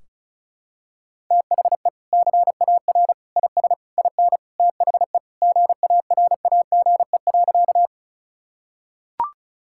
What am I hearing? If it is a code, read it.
Morse code, heard as a keyed tone, about 36 wpm, 700 Hz. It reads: THE CAR IS IN THE GARAGE.
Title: Sentences from Top 1600 Words 35wpm